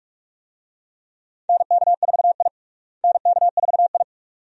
Morse code, heard as a keyed tone, about 47 wpm, 700 Hz. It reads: NK4I NK4I
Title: USA Call Signs - 4 Characters - Repeated 2x - 45wpm